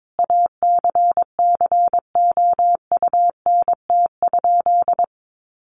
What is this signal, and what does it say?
Morse code, heard as a keyed tone, about 22 wpm, 700 Hz. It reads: ACCOUNT?
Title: Sentences from Top 1500 Words 22wpm